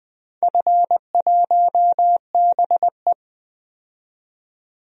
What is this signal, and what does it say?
Morse code, heard as a keyed tone, about 20 wpm, 700 Hz. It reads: F1BE
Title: DXCC-100 Call Signs - 4 Characters - 20wpm